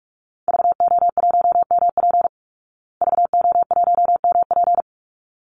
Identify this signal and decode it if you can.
Morse code, heard as a keyed tone, about 44 wpm, 700 Hz. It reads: VO1MP VO1MP
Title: Call Signs Repeated 3x - 45wpm